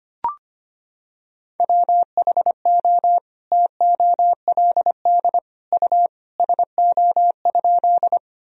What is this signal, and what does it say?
Morse code, heard as a keyed tone, about 25 wpm, 700 Hz. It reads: WHO TOLD U SO?
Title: Sentences from Top 400 Words and Abbreviations 25wpm